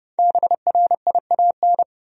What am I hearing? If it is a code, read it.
Morse code, heard as a keyed tone, about 30 wpm, 700 Hz. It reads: BRIAN
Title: USA Common Names - Max 5 Letters - 30wpm